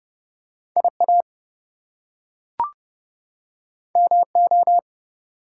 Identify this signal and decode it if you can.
Morse code, heard as a keyed tone, about 30 wpm, 700 Hz. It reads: IA MO
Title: US States and Canadian Provinces 30wpm